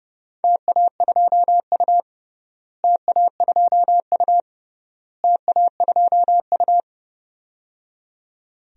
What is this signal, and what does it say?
Morse code, heard as a keyed tone, about 30 wpm, 700 Hz. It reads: TA2U TA2U TA2U
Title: DXCC-100 Call Signs - 4 Characters - Repeated 3x - 30wpm